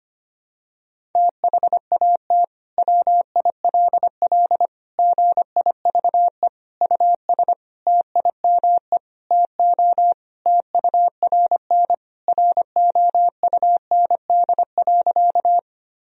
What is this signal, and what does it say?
Morse code, heard as a keyed tone, about 25 wpm, 700 Hz. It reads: THAT WILL GIVE US TIME TO TURN ROUND.
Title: Sentences from Top 1000 Words - Review Part 1 - 25wpm